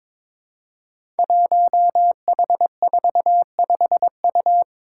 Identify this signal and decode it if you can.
Morse code, heard as a keyed tone, about 22 wpm, 700 Hz. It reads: 1H45U
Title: Five Challenging Letter-Number Combinations 22wpm